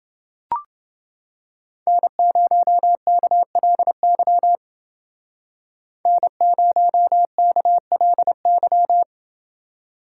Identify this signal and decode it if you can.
Morse code, heard as a keyed tone, about 28 wpm, 700 Hz. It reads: N0KLY N0KLY